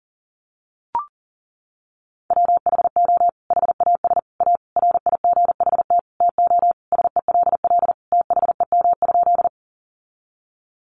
Morse code, heard as a keyed tone, about 37 wpm, 700 Hz. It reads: WHO HAS A RIGHT TO SELL THEM?
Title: Sentences from Top 1100 Words 40wpm